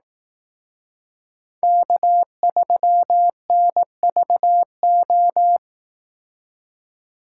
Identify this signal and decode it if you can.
Morse code, heard as a keyed tone, about 18 wpm, 700 Hz. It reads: K3NVO
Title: Speed Racing - USA 1x3 Call Signs 15wpm